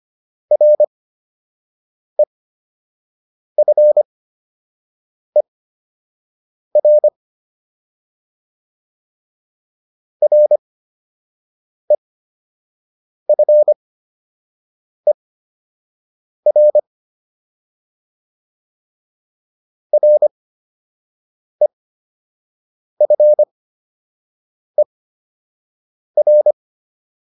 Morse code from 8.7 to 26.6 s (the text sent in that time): REFER REFER